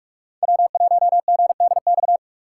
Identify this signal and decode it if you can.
Morse code, heard as a keyed tone, about 41 wpm, 700 Hz. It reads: W1GDX